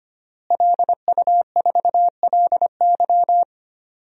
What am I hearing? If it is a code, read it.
Morse code, heard as a keyed tone, about 25 wpm, 700 Hz. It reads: LU4LY
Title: DXCC-100 Call Signs - 5 Characters - Repeated 3x - 25wpm